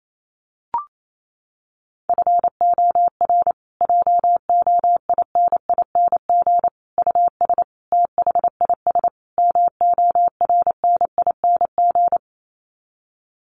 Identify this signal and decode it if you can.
Morse code, heard as a keyed tone, about 28 wpm, 700 Hz. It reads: FOR JOINING US THIS MORNING
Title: Sets of 5 Words Encore 28wpm